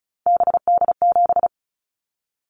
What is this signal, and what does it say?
Morse code, heard as a keyed tone, about 36 wpm, 700 Hz. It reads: BD7
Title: Callsign Prefixes for Top 100 DXCC - 35wpm